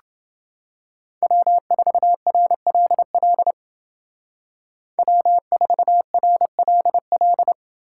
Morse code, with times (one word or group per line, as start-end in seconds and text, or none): 1.2-3.5 W4RLL
5.0-7.5 W4RLL